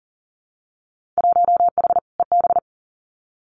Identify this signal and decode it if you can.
Morse code, heard as a keyed tone, about 42 wpm, 700 Hz. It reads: 1H EB
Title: Winter Field Day Exchanges 40wpm